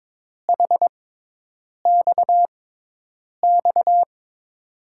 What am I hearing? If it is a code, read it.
Morse code, heard as a keyed tone, about 22 wpm, spaced as for 8 wpm, 700 Hz. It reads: HXX
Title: USA Call Signs - 6 Characters - 6x Character Spacing - 22wpm